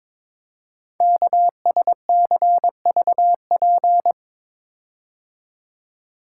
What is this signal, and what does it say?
Morse code, heard as a keyed tone, about 22 wpm, 700 Hz. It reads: KSCVP
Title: Code Groups - 5 Letters - 22wpm